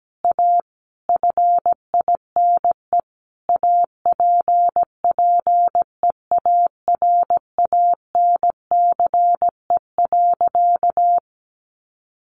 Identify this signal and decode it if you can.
Morse code, heard as a keyed tone, about 17 wpm, 700 Hz. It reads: A FINE APPEARANCE.
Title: Sentences from Top 1100 Words 17wpm